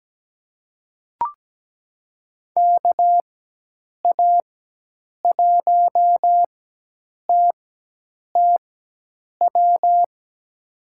Morse code, heard as a keyed tone, about 17 wpm, spaced as for 8 wpm, 700 Hz. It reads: KA1TTW
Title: USA Call Signs - 6 Characters - 4x Character Spacing - 17wpm